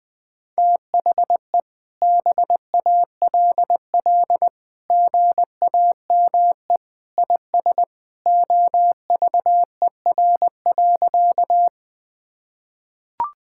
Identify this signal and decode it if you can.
Morse code, heard as a keyed tone, about 20 wpm, 700 Hz. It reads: THE BALL GAME IS OVER.
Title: Sentences from Top 1000 Words - Review Part 6 - 20wpm